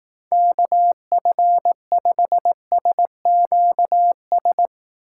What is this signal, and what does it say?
Morse code, heard as a keyed tone, about 18 wpm, 700 Hz. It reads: KF5SQS
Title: Speed Racing - USA 2x3 Call Signs 15wpm